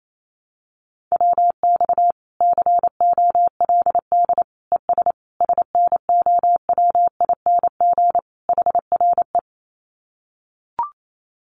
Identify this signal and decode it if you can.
Morse code, heard as a keyed tone, about 28 wpm, 700 Hz. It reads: WX COLD ES SNOWING HRE
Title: Common QSO Phrases 28wpm